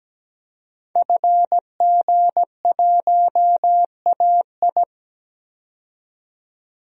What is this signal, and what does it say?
Morse code, heard as a keyed tone, about 17 wpm, 700 Hz. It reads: FG1AI
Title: DXCC-100 Call Signs - 5 Characters - 17wpm